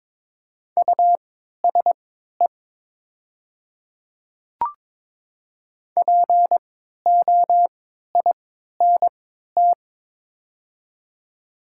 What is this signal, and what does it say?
Morse code, heard as a keyed tone, about 22 wpm, spaced as for 12 wpm, 700 Hz. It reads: USE POINT
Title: 3x Character Spacing - Top 200 Words - 22wpm